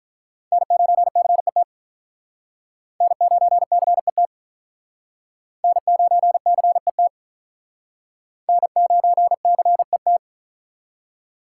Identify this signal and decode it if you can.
Morse code, heard as a keyed tone, about 43 wpm, 700 Hz. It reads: N9CET N9CET N9CET N9CET
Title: Speed Racing - USA Call Signs 35wpm